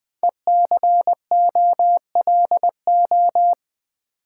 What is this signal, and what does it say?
Morse code, heard as a keyed tone, about 20 wpm, 700 Hz. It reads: ECOLO
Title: Build A Word - 5 Letters At A Time - 10-Letter Words - 20wpm